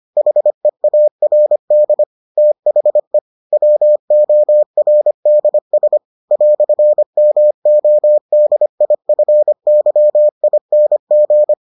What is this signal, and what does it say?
Morse code, heard as a keyed tone, about 25 wpm, 600 Hz. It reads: HEARD THE WORDS "MODIFYING